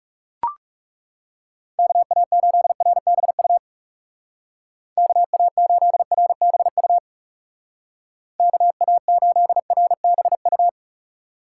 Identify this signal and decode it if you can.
Morse code, heard as a keyed tone, about 37 wpm, 700 Hz. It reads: KA8RBU KA8RBU KA8RBU
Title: Speed Racing - USA 2x3 Call Signs 30wpm